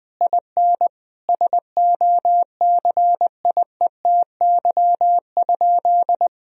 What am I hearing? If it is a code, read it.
Morse code, heard as a keyed tone, about 21 wpm, 700 Hz. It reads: IN SOCIETY?
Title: Sentences from Top 1000 Words - Review Part 4 - 20wpm